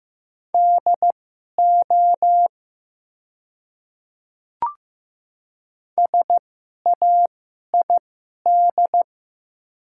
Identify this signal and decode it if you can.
Morse code, heard as a keyed tone, about 15 wpm, spaced as for 11 wpm, 700 Hz. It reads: DO SAID
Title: 2x Character Spacing - Top 100 Words - 15wpm